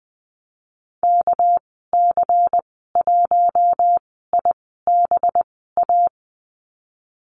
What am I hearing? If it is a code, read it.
Morse code, heard as a keyed tone, about 20 wpm, spaced as for 14 wpm, 700 Hz. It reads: KC1IBA
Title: 245 Lesson 39g - Letter z - Call Signs 20wpm